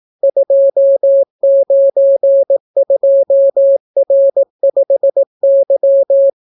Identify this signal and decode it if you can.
Morse code, heard as a keyed tone, about 18 wpm, 550 Hz. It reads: 292R5Y